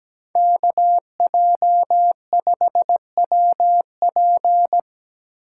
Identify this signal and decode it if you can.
Morse code, heard as a keyed tone, about 17 wpm, 700 Hz. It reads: KJ5WP
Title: USA Call Signs - 5 Characters - 2x Character Spacing - 17wpm